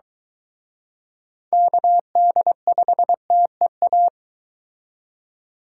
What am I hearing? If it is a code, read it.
Morse code, heard as a keyed tone, about 23 wpm, 700 Hz. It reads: KD5TEA